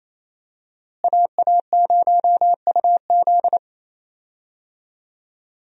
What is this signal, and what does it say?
Morse code, heard as a keyed tone, about 28 wpm, 700 Hz. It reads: AA0UZ
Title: USA Call Signs - 5 Characters - Repeated 2x - 28wpm